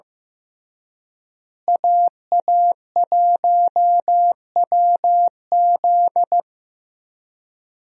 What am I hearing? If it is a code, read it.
Morse code, heard as a keyed tone, about 15 wpm, 700 Hz. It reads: AA1WZ
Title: USA Call Signs - 5 Characters - Repeated 2x - 15wpm